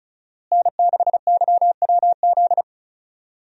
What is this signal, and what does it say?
Morse code, heard as a keyed tone, about 36 wpm, 700 Hz. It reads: N6YWZ
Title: USA Call Signs - 5 Characters - Repeated 2x - 35wpm